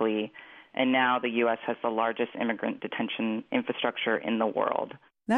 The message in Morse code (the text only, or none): none